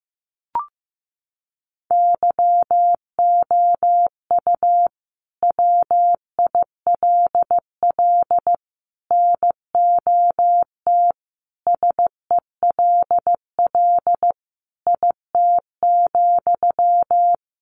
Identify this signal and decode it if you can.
Morse code, heard as a keyed tone, about 15 wpm, 700 Hz. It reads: YOU WILL NOT SELL IT,
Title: Sentences from Top 1100 Words 15wpm